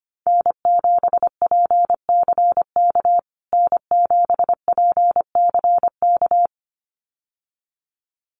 Morse code, heard as a keyed tone, about 25 wpm, 700 Hz. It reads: N7PCK N7PCK
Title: USA Call Signs Repeated 2x 25wpm